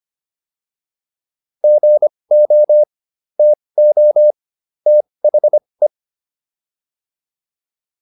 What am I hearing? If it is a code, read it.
Morse code, heard as a keyed tone, about 25 wpm, spaced as for 20 wpm, 600 Hz. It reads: GO TO THE
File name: WD202 20